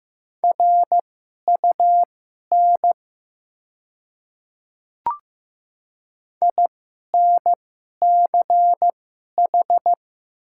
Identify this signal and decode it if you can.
Morse code, heard as a keyed tone, about 15 wpm, spaced as for 11 wpm, 700 Hz. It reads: RUN INCH
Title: 2x Character Spacing - Top 500 Words - 15wpm